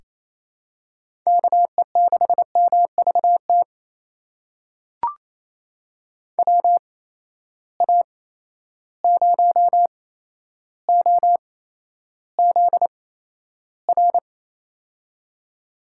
Morse code, heard as a keyed tone, about 28 wpm, spaced as for 12 wpm, 700 Hz. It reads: KE6MVT WA0OZR
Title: USA Call Signs - 6 Characters - 8x Character Spacing - 28wpm